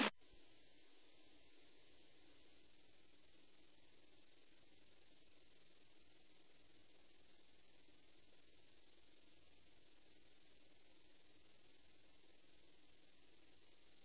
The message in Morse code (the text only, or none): none